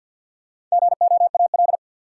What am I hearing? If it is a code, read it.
Morse code, heard as a keyed tone, about 53 wpm, 700 Hz. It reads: GOAL
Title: Sentences from Top 1400 Words - New Word Drill Alt - 50wpm